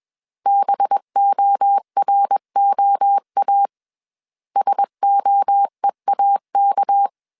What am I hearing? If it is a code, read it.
Morse code, heard as a keyed tone, about 21 wpm, 800 Hz. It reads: BOROA SOEAK
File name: CW / DET / 1 6 1500C S B 20wpm 2048